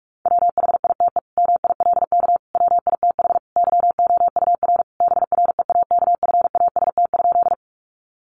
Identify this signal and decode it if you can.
Morse code, heard as a keyed tone, about 44 wpm, 700 Hz. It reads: WHITE MILK WITH YOUR BREAKFAST?
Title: Sentences from Top 1500 Words 45wpm